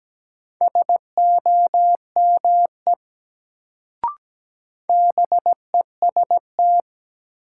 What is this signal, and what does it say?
Morse code, heard as a keyed tone, about 17 wpm, 700 Hz. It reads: SOME BEST
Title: Sets of 2 Words - New Word Drill - 17wpm